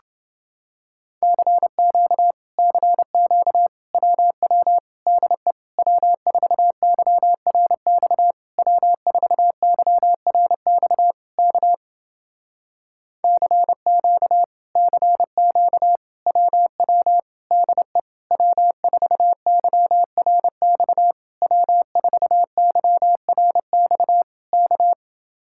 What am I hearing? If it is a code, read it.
Morse code, heard as a keyed tone, about 29 wpm, 700 Hz. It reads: CQ CQ WW DE W4YRX W4YRX K CQ CQ WW DE W4YRX W4YRX K